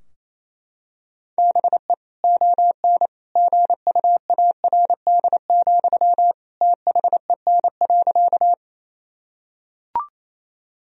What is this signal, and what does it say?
Morse code, heard as a keyed tone, about 27 wpm, 700 Hz. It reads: BE ON GUARD, THEN.
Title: Sentences from Top 1000 Words - Review Part 4 - 28wpm